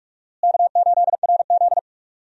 Morse code, heard as a keyed tone, about 41 wpm, 700 Hz. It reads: K8RZ
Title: USA Call Signs - 4 Characters - Repeated 3x - 45wpm